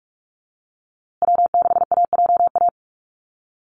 Morse code, heard as a keyed tone, about 44 wpm, 690 Hz. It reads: W6AJA
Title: USA Call Signs Repeated 2x 45wpm